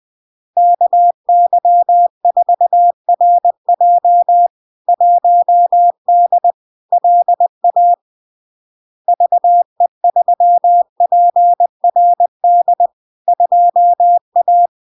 Morse code, heard as a keyed tone, about 20 wpm, 700 Hz. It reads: KY4RJ 1D LA VE3PRD 2A
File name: LARK-FD-2022-Log-in-CW-20wpm-700-Hz